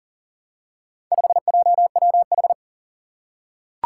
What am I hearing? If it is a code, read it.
Morse code, heard as a keyed tone, about 38 wpm, 700 Hz. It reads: 5JWH
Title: Four Challenging Letter-Number Combinations 40wpm